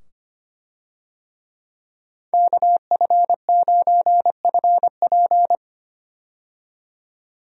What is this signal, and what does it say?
Morse code, heard as a keyed tone, about 25 wpm, 700 Hz. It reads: KF9FP
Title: USA Call Signs - 5 Characters - 25wpm